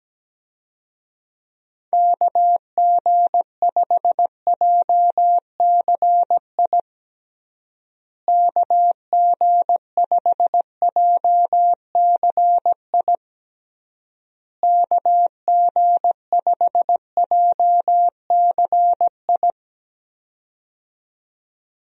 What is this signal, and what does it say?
Morse code, heard as a keyed tone, about 17 wpm, 700 Hz. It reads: KG5JCI KG5JCI KG5JCI
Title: USA Call Signs - 6 Characters - Repeated 3x - 17wpm